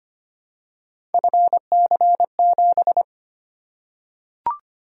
Morse code, heard as a keyed tone, about 25 wpm, 700 Hz. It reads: FC7
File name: Three Letter-Number Combinations - 25wpm